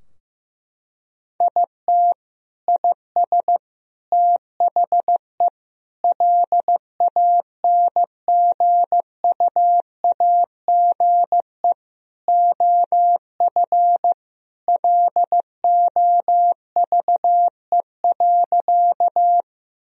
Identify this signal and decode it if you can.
Morse code, heard as a keyed tone, about 15 wpm, 700 Hz. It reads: IT IS THE LANGUAGE OF LOVE.
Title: Sentences from Top 1000 Words - Review Part 3 - 15wpm